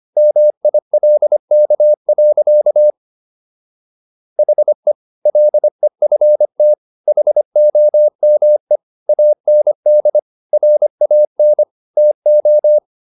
Morse code, heard as a keyed tone, about 25 wpm, 600 Hz. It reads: MILK. HE LEFT HOME AND RAN TO